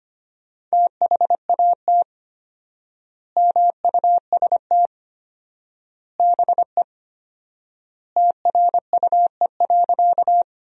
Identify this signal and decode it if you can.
Morse code, heard as a keyed tone, about 25 wpm, 700 Hz. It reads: THAT MUST BE TRUE.
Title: 4x Word-Spacing - Sentences from Top 400 Words - 25wpm